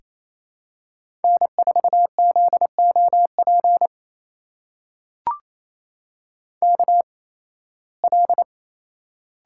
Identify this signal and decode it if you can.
Morse code, heard as a keyed tone, about 28 wpm, spaced as for 18 wpm, 700 Hz. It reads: N4ZOP KL
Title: USA Call Signs - 8x Character Spacing - 28wpm